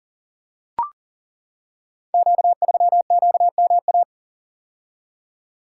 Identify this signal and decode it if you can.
Morse code, heard as a keyed tone, about 42 wpm, 700 Hz. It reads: Q3QMA
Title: Code Groups - 5 Letter:Numbers - 40wpm